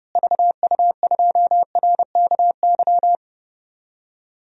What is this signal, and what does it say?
Morse code, heard as a keyed tone, about 30 wpm, 700 Hz. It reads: VU2RKY
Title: DXCC-100 Call Signs - 6 Characters - Repeated 3x - 30wpm